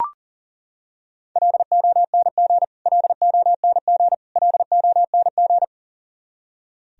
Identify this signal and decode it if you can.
Morse code, heard as a keyed tone, about 40 wpm, 700 Hz. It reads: LONG LONG LONG
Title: Top 100 Words Repeated 40wpm